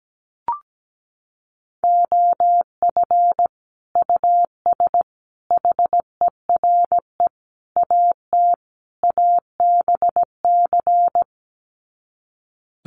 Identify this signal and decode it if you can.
Morse code, heard as a keyed tone, about 17 wpm, 700 Hz. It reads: OF US HERE AT ABC
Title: Sets of 5 Words 17wpm